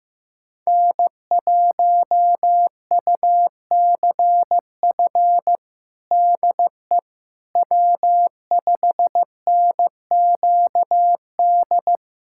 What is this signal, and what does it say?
Morse code, heard as a keyed tone, about 15 wpm, 700 Hz. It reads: N1UCF DE W5NQD
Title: USA Callsign DE Callsign - 5 Characters - 15wpm